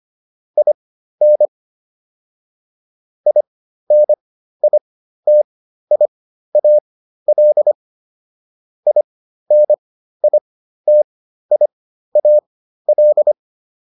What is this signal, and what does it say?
Morse code, heard as a keyed tone, about 25 wpm, spaced as for 13 wpm, 600 Hz. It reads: IN INITIAL INITIAL
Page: intermediate-practice-files